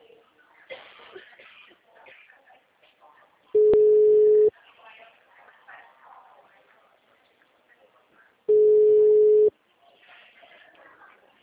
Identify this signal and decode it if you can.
Ringback tone (Europe, 425 Hz), with speech.